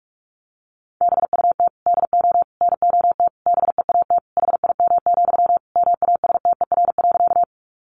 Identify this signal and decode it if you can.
Morse code, heard as a keyed tone, about 44 wpm, 700 Hz. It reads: BUT DO NOT BEAT HIM, MASTER.
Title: Sentences from Top 1300 Words 45wpm